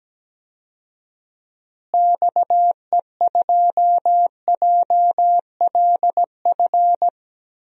Morse code, heard as a keyed tone, about 17 wpm, 700 Hz. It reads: XE2JLF
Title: DXCC-100 Call Signs - 6 Characters - Repeated 3x - 17wpm